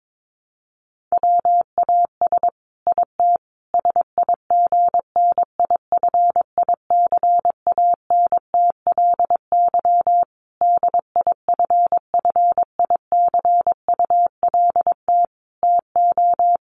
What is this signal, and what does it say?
Morse code, heard as a keyed tone, about 22 wpm, 700 Hz. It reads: WAS IT SIGNIFICANTLY DIFFICULT TO
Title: Sentences from Top 1500 Words 22wpm